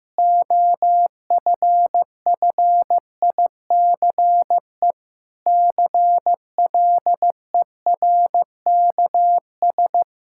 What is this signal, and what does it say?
Morse code, heard as a keyed tone, about 15 wpm, 700 Hz. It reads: OFFICE CLERKS
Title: Pangram Challenge v2 - 15wpm